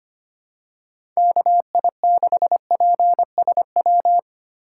Morse code, heard as a keyed tone, about 25 wpm, 700 Hz. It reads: KI6PSW
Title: USA Call Signs - 6 Characters - 2x Character Spacing - 25wpm